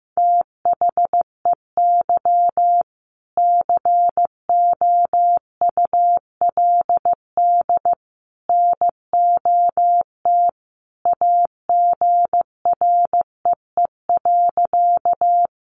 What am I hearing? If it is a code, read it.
Morse code, heard as a keyed tone, about 15 wpm, 700 Hz. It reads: THEY COULD NOT AGREE.
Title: Sentences from Top 1100 Words 15wpm